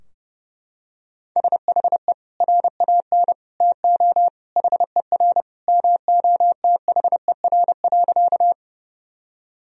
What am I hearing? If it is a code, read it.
Morse code, heard as a keyed tone, about 30 wpm, 700 Hz. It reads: SHE RAN TO HER MOTHER.
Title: Sentences from Top 1000 Words - Review Part 3 - 30wpm